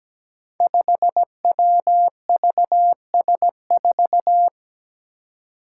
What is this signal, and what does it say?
Morse code, heard as a keyed tone, about 17 wpm, 700 Hz. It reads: 5WVS4
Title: Five Challenging Letter-Number Combinations 17wpm